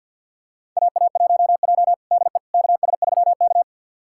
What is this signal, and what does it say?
Morse code, heard as a keyed tone, about 46 wpm, 690 Hz. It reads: AA1J DE KS3K
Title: USA Callsign DE Callsign - 4 Characters - 50wpm